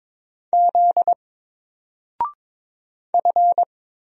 Morse code, heard as a keyed tone, about 22 wpm, 700 Hz. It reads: Z F
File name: Single-Letter-Number-22wpm